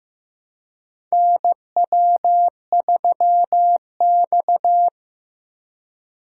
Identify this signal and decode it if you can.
Morse code, heard as a keyed tone, about 15 wpm, 700 Hz. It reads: NW3X
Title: USA Call Signs - 3x Character Spacing - 15wpm